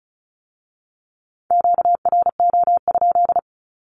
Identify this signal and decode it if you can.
Morse code, heard as a keyed tone, about 36 wpm, 700 Hz. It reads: QRO?
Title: Q-Codes with Spoken Meaning - 35wpm